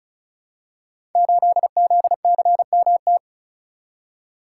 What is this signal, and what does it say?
Morse code, heard as a keyed tone, about 36 wpm, 700 Hz. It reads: 8ZCMT
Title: Code Groups - 5 Letter:Numbers - 35wpm